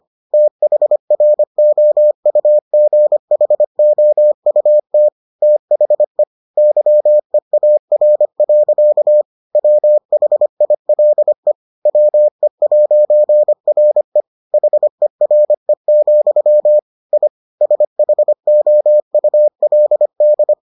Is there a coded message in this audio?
Yes: Morse code from 0.3 to 20.5 s, heard as a keyed tone, about 25 wpm, 600 Hz. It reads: THROUGHOUT THE YEAR. WHILE WE'RE HERE, I SHOULD